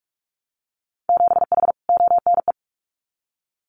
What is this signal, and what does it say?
Morse code, heard as a keyed tone, about 46 wpm, 700 Hz. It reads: 7H ONE
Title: Winter Field Day Exchanges Encore 45wpm